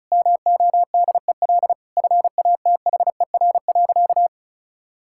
Morse code, heard as a keyed tone, about 36 wpm, 700 Hz. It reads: MODEL FATHER.
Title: Sentences from Top 1000 Words - Review Part 5 - 35wpm